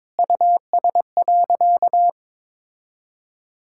Morse code, heard as a keyed tone, about 22 wpm, 700 Hz. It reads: US.
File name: Sentences with 9-Word Phrases - 22wpm